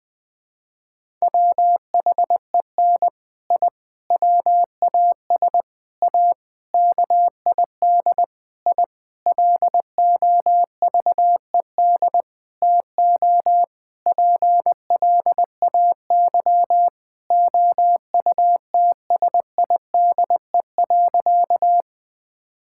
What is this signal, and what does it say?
Morse code, heard as a keyed tone, about 21 wpm, 700 Hz. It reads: WHEN I WAS A KID I LOVED TO PLAY OUTSIDE.
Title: Sentences with 5-Word Phrases - 20wpm